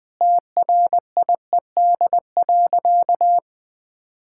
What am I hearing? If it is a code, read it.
Morse code, heard as a keyed tone, about 20 wpm, 700 Hz. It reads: TRIED.
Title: Sentences with 8-Word Phrases - 20wpm